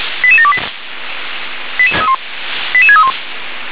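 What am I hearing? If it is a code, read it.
Five-tone selective call (ZVEI), heard as quick runs of five tones, tones of about 70 ms. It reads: 88728 88631
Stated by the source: ZVEI five-tone selective call, received on SDR.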